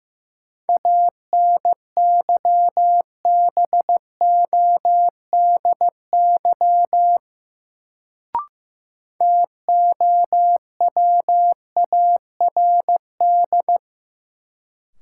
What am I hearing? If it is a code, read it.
Morse code, heard as a keyed tone, about 15 wpm, 700 Hz. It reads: ANYBODY TOWARD
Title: Sets of 5 Words - New Word Drill - 15wpm